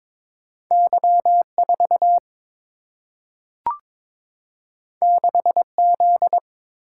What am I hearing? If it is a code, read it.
Morse code, heard as a keyed tone, about 22 wpm, 700 Hz. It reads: Y4 6Z
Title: Single-Letter Single-Number 22wpm